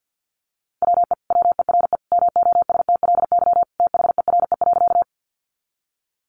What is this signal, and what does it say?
Morse code, heard as a keyed tone, about 50 wpm, 700 Hz. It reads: WE WERE MOSTLY THERE.
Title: Sentences from Top 1300 Words 50wpm